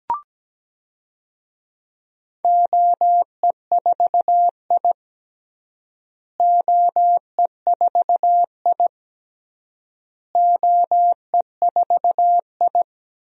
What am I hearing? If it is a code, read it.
Morse code, heard as a keyed tone, about 17 wpm, 700 Hz. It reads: OE4I OE4I OE4I
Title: DXCC-100 Call Signs - 4 Characters - Repeated 3x - 17wpm